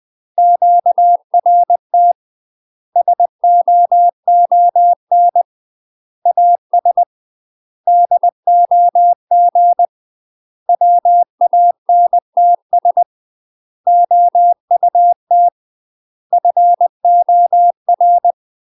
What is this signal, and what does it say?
Morse code, heard as a keyed tone, about 20 wpm, 700 Hz. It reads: QRT SOON AS DOG WANTS OUT FOR
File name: UK MASTER QSOs TRAINING / 20WPM / UK MASTER QSOs 0051